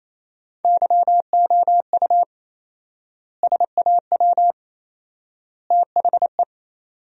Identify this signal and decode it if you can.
Morse code, heard as a keyed tone, about 28 wpm, 700 Hz. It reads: YOU SAW THE
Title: 4x Word-Spacing - Sentences from Top 400 Words - 28wpm